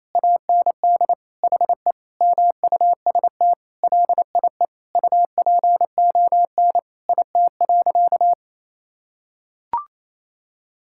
Morse code, heard as a keyed tone, about 27 wpm, 700 Hz. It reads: AND HE MUST LIE UPON IT.
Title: Sentences from Top 1600 Words 28wpm